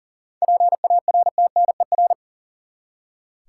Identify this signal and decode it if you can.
Morse code, heard as a keyed tone, about 40 wpm, 700 Hz. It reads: PARTNER
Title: Seven Letter Words - 40wpm